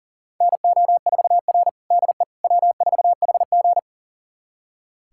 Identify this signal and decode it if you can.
Morse code, heard as a keyed tone, about 40 wpm, 700 Hz. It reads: NO4R DE W4HG
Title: USA Callsign DE Callsign - 4 Characters - 40wpm